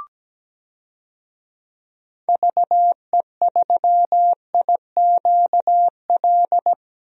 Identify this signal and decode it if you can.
Morse code, heard as a keyed tone, about 17 wpm, 700 Hz. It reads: VE3IQL
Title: DXCC-100 Call Signs - 6 Characters - 17wpm